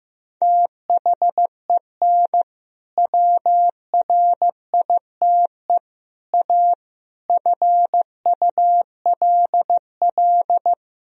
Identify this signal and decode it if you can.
Morse code, heard as a keyed tone, about 15 wpm, 700 Hz. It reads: THEN WRITE A FULL
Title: Sentences from Top 1000 Words - Review Part 3 - 15wpm